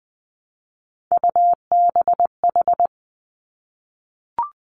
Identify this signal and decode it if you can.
Morse code, heard as a keyed tone, about 20 wpm, 700 Hz. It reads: UBH
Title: Three Challenging Letter-Number Combinations 20wpm